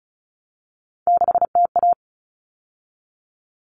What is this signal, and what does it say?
Morse code, heard as a keyed tone, about 34 wpm, 700 Hz. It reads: 6TA
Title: All Cut Numbers - 3-Digit 35wpm